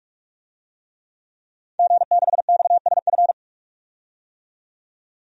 Morse code, heard as a keyed tone, about 46 wpm, 700 Hz. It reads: G6XSF
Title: DXCC-100 Call Signs - 5 Characters - 45wpm